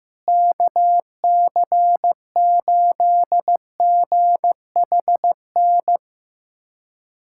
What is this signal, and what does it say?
Morse code, heard as a keyed tone, about 15 wpm, 700 Hz. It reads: KC8GHN
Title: USA Callsign DE Callsign - 6 Characters - 15wpm